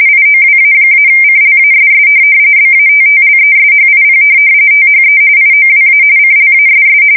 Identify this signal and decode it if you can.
RTTY, about 45.45 baud. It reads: IOWA-CLASS BATTLESHIP THE IOWA CLASS
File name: fsk170 / USS-Missouri